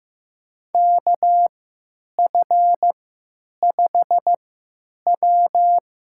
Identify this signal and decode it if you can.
Morse code, heard as a keyed tone, about 15 wpm, spaced as for 9 wpm, 700 Hz. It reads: KF5W
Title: USA Call Signs - 6 Characters - 3x Character Spacing - 15wpm